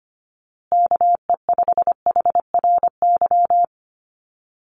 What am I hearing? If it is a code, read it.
Morse code, heard as a keyed tone, about 25 wpm, 700 Hz. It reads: KE5HRY
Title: USA Call Signs Repeated 2x 25wpm